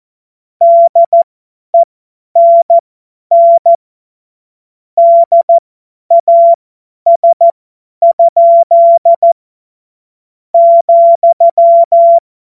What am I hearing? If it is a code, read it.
Morse code, heard as a keyed tone, about 14 wpm, spaced as for 9 wpm, 680 Hz. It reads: DENN DAS?,
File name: Morsekurs / Witze / Witz 07. Februar 2023 50 F70